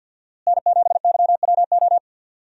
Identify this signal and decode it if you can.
Morse code, heard as a keyed tone, about 52 wpm, 700 Hz. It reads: N7YWO
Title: USA Call Signs - 5 Characters - Repeated 3x - 50wpm